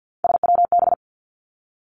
Morse code, heard as a keyed tone, about 50 wpm, 700 Hz. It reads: SWB
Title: Three Challenging Letter-Number Combinations 50wpm